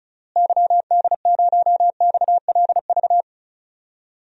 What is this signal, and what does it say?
Morse code, heard as a keyed tone, about 34 wpm, 700 Hz. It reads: YD0XLV
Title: DXCC-100 Call Signs - 6 Characters - Repeated 3x - 35wpm